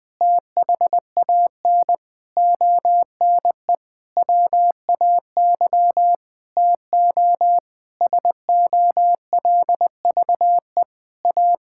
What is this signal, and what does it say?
Morse code, heard as a keyed tone, about 20 wpm, 700 Hz. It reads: THAN ONE WAY TO SOLVE A
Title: Sentences with 9-Word Phrases - 20wpm